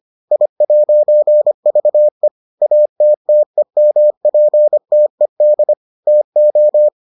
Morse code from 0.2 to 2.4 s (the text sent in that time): I'VE